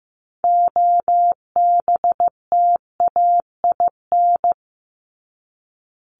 Morse code, heard as a keyed tone, about 15 wpm, 700 Hz. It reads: OBTAIN
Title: ED Suffix - Top 100 Words with Base Words 15wpm